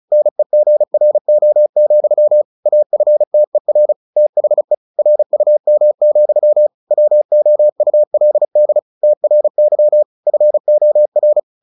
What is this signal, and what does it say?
Morse code, heard as a keyed tone, about 35 wpm, 600 Hz. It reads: NEGRO, AFTER THE RUM, WOULD TRY FOR
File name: the old man and the sea 2 35wpm